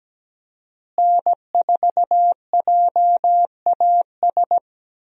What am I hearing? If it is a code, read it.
Morse code, heard as a keyed tone, about 17 wpm, 700 Hz. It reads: N4JAS
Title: USA Call Signs - 5 Characters - Repeated 3x - 17wpm